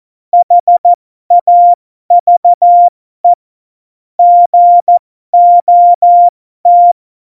Morse code, heard as a keyed tone, about 14 wpm, spaced as for 12 wpm, 700 Hz. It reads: HAVE GOT
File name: Morsekurs / Witze